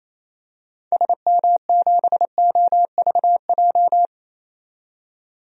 Morse code, heard as a keyed tone, about 28 wpm, 700 Hz. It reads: SM7OVJ
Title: DXCC-100 Call Signs - 6 Characters - 28wpm